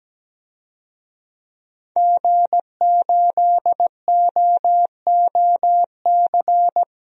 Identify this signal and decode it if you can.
Morse code, heard as a keyed tone, about 17 wpm, 700 Hz. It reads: G8OOC
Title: DXCC-100 Call Signs - 5 Characters - Repeated 3x - 17wpm